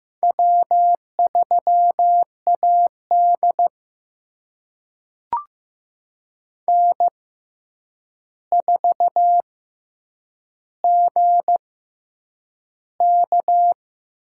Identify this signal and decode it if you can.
Morse code, heard as a keyed tone, about 15 wpm, spaced as for 8 wpm, 700 Hz. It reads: W3AD N4GK
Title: USA Call Signs - 6x Character Spacing - 15wpm